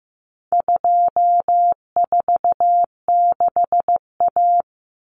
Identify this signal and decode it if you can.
Morse code, heard as a keyed tone, about 15 wpm, 700 Hz. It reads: 246A
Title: Common Cut Numbers - 4-Digit 15wpm